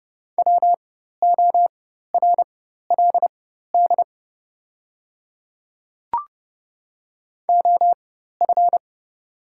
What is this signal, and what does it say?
Morse code, heard as a keyed tone, about 30 wpm, spaced as for 14 wpm, 700 Hz. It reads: WORLD OF
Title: 4x Character Spacing - Top 200 Words - 30wpm